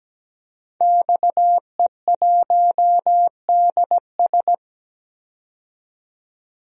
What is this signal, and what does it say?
Morse code, heard as a keyed tone, about 17 wpm, 700 Hz. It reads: XE1DS
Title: DXCC-100 Call Signs - 5 Characters - 17wpm